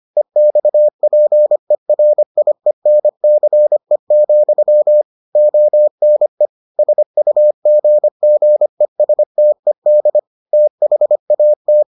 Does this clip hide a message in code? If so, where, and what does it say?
Yes: Morse code from 0.2 to 11.8 s, heard as a keyed tone, about 25 wpm, 600 Hz. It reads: EXPERIENCE, ONE SUGGESTED THAT